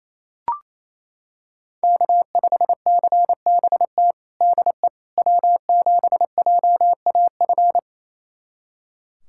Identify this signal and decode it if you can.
Morse code, heard as a keyed tone, about 28 wpm, 700 Hz. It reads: K5CBT DE W7JAF
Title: USA Callsign DE Callsign - 5 Characters - 28wpm